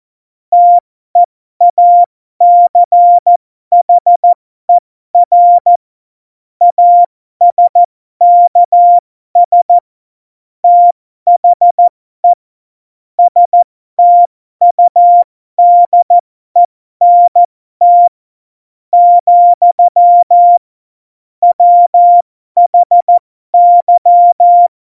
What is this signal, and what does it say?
Morse code, heard as a keyed tone, about 14 wpm, spaced as for 12 wpm, 700 Hz. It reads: TEACHER ASKS THE STUDENT, WHY
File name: Morsekurs / Witze